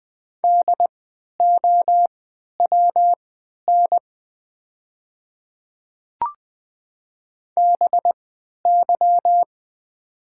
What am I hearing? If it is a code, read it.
Morse code, heard as a keyed tone, about 20 wpm, spaced as for 11 wpm, 700 Hz. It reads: DOWN BY
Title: 3x Character Spacing - Top 100 Words - 20wpm